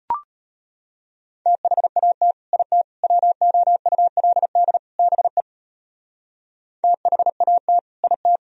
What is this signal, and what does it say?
Morse code, heard as a keyed tone, about 33 wpm, 690 Hz. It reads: THAT IT WOULD BE THAT IT